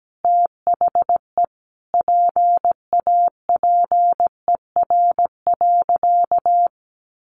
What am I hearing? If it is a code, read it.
Morse code, heard as a keyed tone, about 17 wpm, 700 Hz. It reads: THE PAPER.
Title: Sentences from Top 1300 Words 17wpm